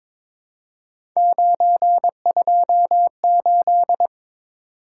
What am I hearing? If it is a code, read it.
Morse code, heard as a keyed tone, about 23 wpm, 700 Hz. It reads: 928
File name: practice / 22wpm / Three Letter-Number Combinations - 22wpm